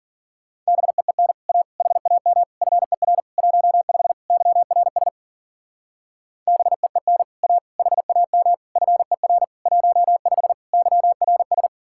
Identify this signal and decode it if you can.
Morse code, heard as a keyed tone, about 52 wpm, 700 Hz. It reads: BEEN A HAM FER 15 YRS BEEN A HAM FER 15 YRS